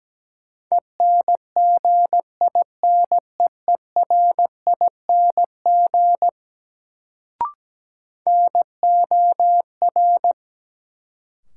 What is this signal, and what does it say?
Morse code, heard as a keyed tone, about 17 wpm, 700 Hz. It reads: ENGINEERING NOR
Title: Sentences from Top 1500 Words - New Word Drill - 17wpm